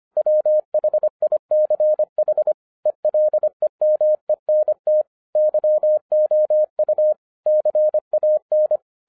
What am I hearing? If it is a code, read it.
Morse code, heard as a keyed tone, about 25 wpm, 610 Hz. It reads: WHICH ELEMENT YOU CAN